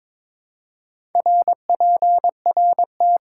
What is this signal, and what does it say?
Morse code, heard as a keyed tone, about 22 wpm, 700 Hz. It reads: RPRT
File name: Top 90 QSO Elements - 4x Character Spacing - 22wpm